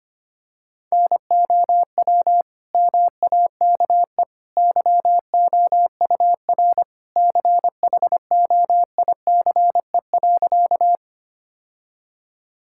Morse code, heard as a keyed tone, about 25 wpm, 700 Hz. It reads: NOW MAKE YOUR CHOICE.
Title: Sentences from Top 1000 Words - Review Part 6 - 25wpm